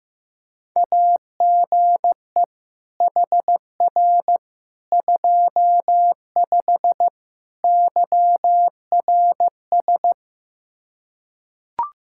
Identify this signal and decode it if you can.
Morse code, heard as a keyed tone, about 15 wpm, 700 Hz. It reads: AGE HR 25 YRS
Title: K1USN SST QSO Practice 15wpm